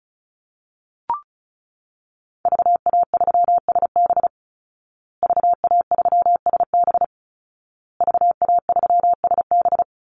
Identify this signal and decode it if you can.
Morse code, heard as a keyed tone, about 34 wpm, 700 Hz. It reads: VA3SB VA3SB VA3SB
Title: Call Signs Repeated 2x - 35wpm